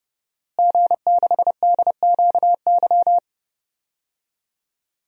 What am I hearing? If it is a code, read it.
Morse code, heard as a keyed tone, about 30 wpm, 700 Hz. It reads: G6DQY
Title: DXCC-100 Call Signs - 5 Characters - Repeated 2x - 30wpm